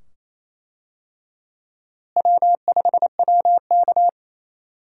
Morse code, heard as a keyed tone, about 28 wpm, 700 Hz. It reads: W5WK
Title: USA Call Signs - 4 Characters - Repeated 2x - 28wpm